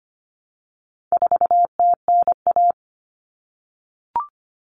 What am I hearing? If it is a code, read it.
Morse code, heard as a keyed tone, about 25 wpm, 700 Hz. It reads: 4TNA
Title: Common Cut Numbers - 4-Digit 25wpm